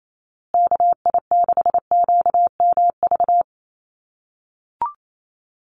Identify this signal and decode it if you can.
Morse code, heard as a keyed tone, about 28 wpm, 700 Hz. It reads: KI6QMV
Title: USA Call Signs Repeated 3x 28wpm